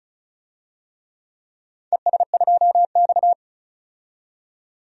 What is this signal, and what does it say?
Morse code, heard as a keyed tone, about 35 wpm, 700 Hz. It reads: ES2X
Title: DXCC-100 Call Signs - 4 Characters - 35wpm